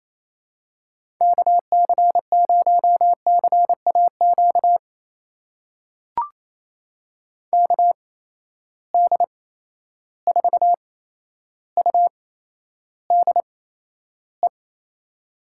Morse code, heard as a keyed tone, about 28 wpm, spaced as for 12 wpm, 700 Hz. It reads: KC0CAQ KD4UDE
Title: USA Call Signs - 6 Characters - 8x Character Spacing - 28wpm